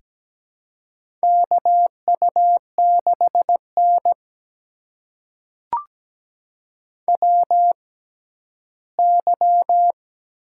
Morse code, heard as a keyed tone, about 17 wpm, spaced as for 12 wpm, 700 Hz. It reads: KU6N WY
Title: USA Call Signs - 4 Characters - 6x Character Spacing - 17wpm